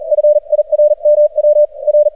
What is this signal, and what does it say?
Morse code, heard as a keyed tone, about 43 wpm, spaced as for 37 wpm, 600 Hz. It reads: FIRMW R
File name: archived / CRSnet